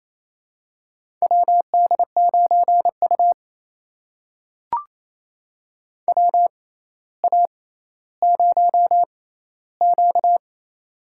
Morse code, heard as a keyed tone, about 28 wpm, spaced as for 14 wpm, 700 Hz. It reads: WD9U WA0Q
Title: USA Call Signs - 4 Characters - 6x Character Spacing - 28wpm